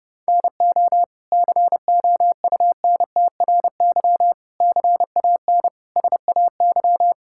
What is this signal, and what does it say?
Morse code, heard as a keyed tone, about 32 wpm, 700 Hz. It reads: NO COUNTRY CAN SAY
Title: Sentences from Top 1000 Words - Review Part 5 - 30wpm